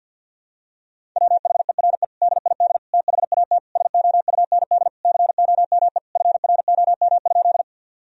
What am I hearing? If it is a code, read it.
Morse code, heard as a keyed tone, about 52 wpm, 700 Hz. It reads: WHERE DID THAT SOUND COME FROM?
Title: Sentences from Top 1000 Words - Review Part 1 - 50wpm